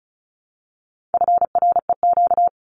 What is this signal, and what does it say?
Morse code, heard as a keyed tone, about 34 wpm, 700 Hz. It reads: FREQ
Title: Abbreviations Repeated 3x - 35wpm